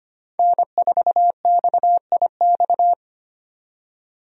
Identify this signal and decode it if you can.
Morse code, heard as a keyed tone, about 25 wpm, 700 Hz. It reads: N4XIX
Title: USA Call Signs - 5 Characters - 25wpm